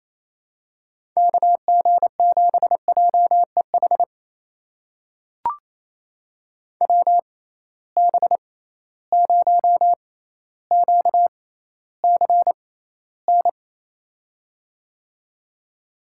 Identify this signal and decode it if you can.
Morse code, heard as a keyed tone, about 28 wpm, spaced as for 14 wpm, 700 Hz. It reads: KG7JEH WB0QCN
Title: USA Call Signs - 6 Characters - 6x Character Spacing - 28wpm